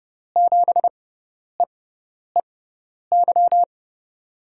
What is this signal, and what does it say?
Morse code, heard as a keyed tone, about 31 wpm, spaced as for 10 wpm, 700 Hz. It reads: 7EEY
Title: USA Call Signs - 6x Character Spacing - 30wpm